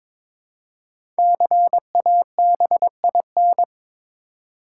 Morse code, heard as a keyed tone, about 22 wpm, 700 Hz. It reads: CABIN
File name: 5 Letter Words - First 3 Letters Given - 22wpm